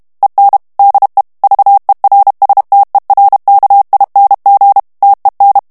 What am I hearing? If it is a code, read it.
Morse code, heard as a keyed tone, about 32 wpm, 800 Hz. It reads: EN DE VERSTERKING TEN
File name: morse mp3 / HOLLANDAIS texte clair 1 vitesse 30 mots minute